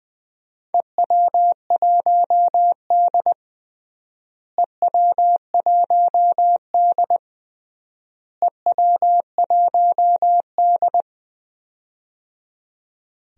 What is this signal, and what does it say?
Morse code, heard as a keyed tone, about 20 wpm, 700 Hz. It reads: EW1D EW1D EW1D
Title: DXCC-100 Call Signs - 4 Characters - Repeated 3x - 20wpm